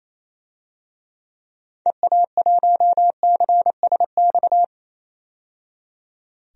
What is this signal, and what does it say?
Morse code, heard as a keyed tone, about 28 wpm, 700 Hz. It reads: EA1CSX
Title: DXCC-100 Call Signs - 6 Characters - 28wpm